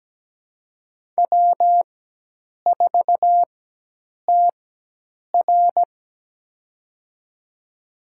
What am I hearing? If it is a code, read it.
Morse code, heard as a keyed tone, about 17 wpm, spaced as for 8 wpm, 700 Hz. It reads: W4TR
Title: USA Call Signs - 4 Characters - 4x Character Spacing - 17wpm